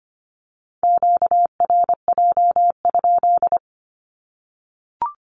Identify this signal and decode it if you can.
Morse code, heard as a keyed tone, about 25 wpm, 700 Hz. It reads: QRJ?
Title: Q-Codes Repeated 3x 25wpm